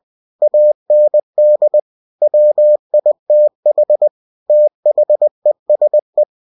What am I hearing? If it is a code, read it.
Morse code, heard as a keyed tone, about 20 wpm, 600 Hz. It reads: AND WITH THESE
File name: cw-books / winnie-the-pooh / winnie-20wpm-page-0095